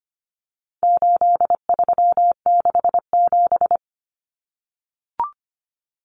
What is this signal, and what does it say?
Morse code, heard as a keyed tone, about 25 wpm, 700 Hz. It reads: 8367
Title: Common Cut Numbers - 4-Digit 25wpm